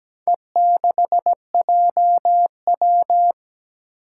Morse code, heard as a keyed tone, about 17 wpm, 700 Hz. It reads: E6JW
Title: Four Challenging Letter-Number Combinations 17wpm